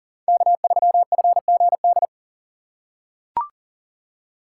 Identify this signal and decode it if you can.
Morse code, heard as a keyed tone, about 40 wpm, 700 Hz. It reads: K3FGD
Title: USA Call Signs - 5 Characters - 2x Character Spacing - 40wpm